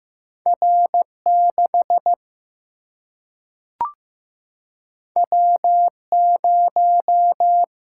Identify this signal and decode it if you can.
Morse code, heard as a keyed tone, about 15 wpm, 700 Hz. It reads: R6 W0
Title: Single-Letter Single-Number 15wpm